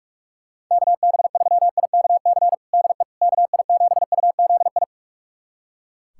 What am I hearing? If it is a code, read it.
Morse code, heard as a keyed tone, about 43 wpm, 690 Hz. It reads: KB3IKC DE KI7UZI
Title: USA Callsign DE Callsign - 6 Characters - 45wpm